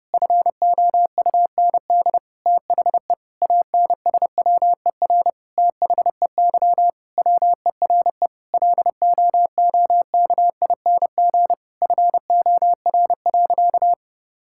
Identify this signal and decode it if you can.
Morse code, heard as a keyed tone, about 30 wpm, 700 Hz. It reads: FOUND THE ANSWER THEY WERE LOOKING FOR.
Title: Sentences from Top 1000 Words - Review Part 6 - 30wpm